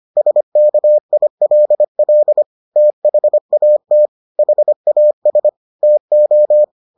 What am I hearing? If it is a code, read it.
Morse code, heard as a keyed tone, about 25 wpm, 600 Hz. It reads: SKILL THAT HAS TO